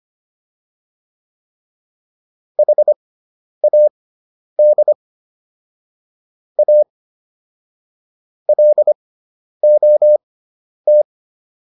Morse code, heard as a keyed tone, about 25 wpm, spaced as for 10 wpm, 600 Hz. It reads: HAD A LOT